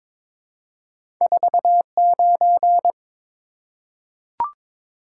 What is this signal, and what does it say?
Morse code, heard as a keyed tone, about 22 wpm, 700 Hz. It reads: 49
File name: practice / 22wpm / Two Letter-Number Combinations - 22wpm